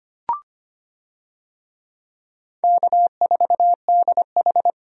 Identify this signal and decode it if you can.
Morse code, heard as a keyed tone, about 25 wpm, 700 Hz. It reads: K4DH
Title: USA Call Signs - 4 Characters - 25wpm